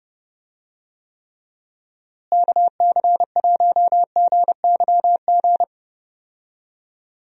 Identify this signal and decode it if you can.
Morse code, heard as a keyed tone, about 30 wpm, 700 Hz. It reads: KC1GYG
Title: USA Call Signs - 6 Characters - Repeated 2x - 30wpm